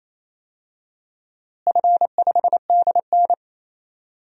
Morse code, heard as a keyed tone, about 28 wpm, 700 Hz. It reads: F5DN
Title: DXCC-100 Call Signs - 4 Characters - Repeated 2x - 28wpm